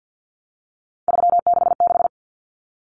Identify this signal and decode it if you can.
Morse code, heard as a keyed tone, about 50 wpm, 700 Hz. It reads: 366
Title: Numbers - 3-Digit 50wpm